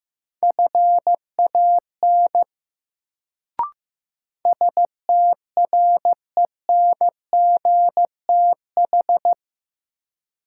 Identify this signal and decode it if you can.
Morse code, heard as a keyed tone, about 15 wpm, 700 Hz. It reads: FAN STRENGTH
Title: Sentences from Top 1000 Words - Review Part 4 - New Word Drill - 15wpm